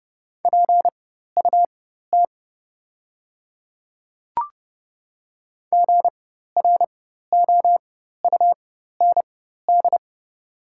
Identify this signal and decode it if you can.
Morse code, heard as a keyed tone, about 30 wpm, spaced as for 14 wpm, 700 Hz. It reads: PUT GROUND
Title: 4x Character Spacing - Top 500 Words - 30wpm